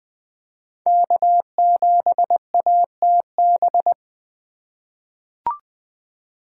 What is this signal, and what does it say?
Morse code, heard as a keyed tone, about 20 wpm, 700 Hz. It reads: K7ATB
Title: USA Call Signs - 4x Character Spacing - 20wpm